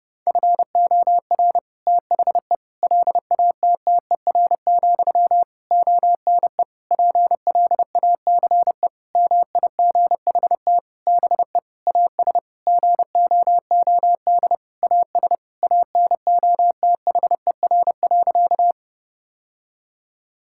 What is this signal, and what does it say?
Morse code, heard as a keyed tone, about 30 wpm, 700 Hz. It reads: FOR THE LATTER, ONE PLACE MIGHT BE AS GOOD AS ANOTHER.
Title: Sentences from Top 1600 Words 30wpm